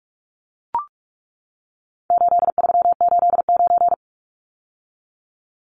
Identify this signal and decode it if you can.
Morse code, heard as a keyed tone, about 44 wpm, 700 Hz. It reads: 8389
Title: Numbers - 4-Digit 45wpm